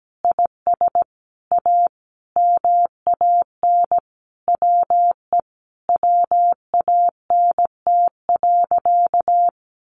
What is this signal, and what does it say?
Morse code, heard as a keyed tone, about 17 wpm, 700 Hz. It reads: IS A MAN WE WANT.
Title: Sentences from Top 200 Words 17wpm